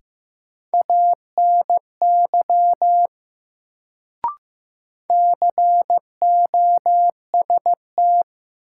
Morse code, heard as a keyed tone, about 15 wpm, 700 Hz. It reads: ANY COST
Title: Sets of 3 Words - New Word Drill - 15wpm